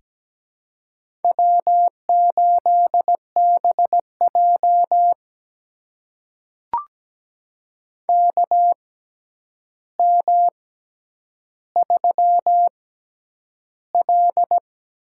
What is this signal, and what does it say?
Morse code, heard as a keyed tone, about 17 wpm, spaced as for 9 wpm, 700 Hz. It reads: W8BJ KM3L
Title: USA Call Signs - 4 Characters - 6x Character Spacing - 17wpm